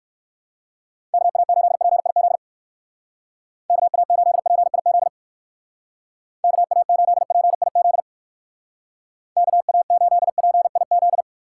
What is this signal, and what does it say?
Morse code, heard as a keyed tone, about 62 wpm, 700 Hz. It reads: KA8PIZ KA8PIZ KA8PIZ KA8PIZ